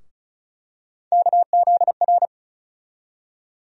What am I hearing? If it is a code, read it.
Morse code, heard as a keyed tone, about 34 wpm, 700 Hz. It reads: KZR
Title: USA Call Sign Postfixes - 3 Letters 35wpm